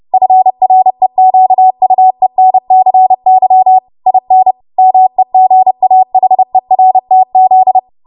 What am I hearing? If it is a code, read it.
Morse code, heard as a keyed tone, about 30 wpm, 750 Hz. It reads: FREQUENCY IN MEGAHERTZ
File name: Morse / 30WPM